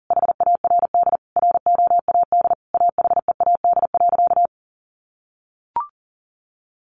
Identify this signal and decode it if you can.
Morse code, heard as a keyed tone, about 40 wpm, 700 Hz. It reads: HARD ROAD AHEAD.
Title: Sentences from Top 1100 Words 40wpm